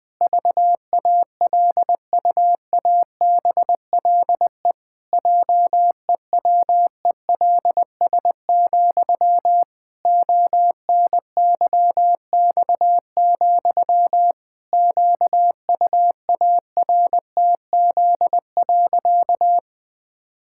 Morse code, heard as a keyed tone, about 20 wpm, 700 Hz. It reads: VALUABLE JEWELS, ONYX, QUARTZ.
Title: Pangram Challenge v2 - 20wpm